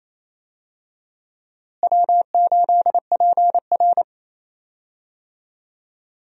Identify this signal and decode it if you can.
Morse code, heard as a keyed tone, about 28 wpm, 700 Hz. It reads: W8PR
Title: USA Call Signs - 4 Characters - Repeated 2x - 28wpm